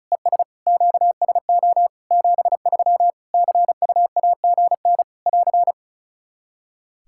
Morse code, heard as a keyed tone, about 36 wpm, 700 Hz. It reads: ES QSO 73 CUAGN <AR>
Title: Practice QSO - Exchange 4 - Ending - 35wpm